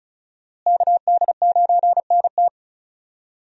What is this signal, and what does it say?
Morse code, heard as a keyed tone, about 34 wpm, 700 Hz. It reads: KD9NT
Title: USA Call Signs - 5 Characters - Repeated 2x - 35wpm